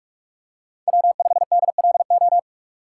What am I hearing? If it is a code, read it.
Morse code, heard as a keyed tone, about 42 wpm, 700 Hz. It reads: W5DLO
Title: USA Call Signs - 6x Character Spacing - 45wpm